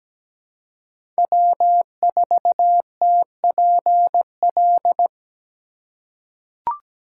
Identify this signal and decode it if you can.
Morse code, heard as a keyed tone, about 17 wpm, 700 Hz. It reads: W4TPL
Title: USA Call Signs - 5 Characters - 2x Character Spacing - 17wpm